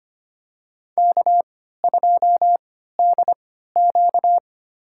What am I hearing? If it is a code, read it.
Morse code, heard as a keyed tone, about 25 wpm, spaced as for 14 wpm, 700 Hz. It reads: K2DQ
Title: USA Call Signs - 4 Characters - 3x Character Spacing - 25wpm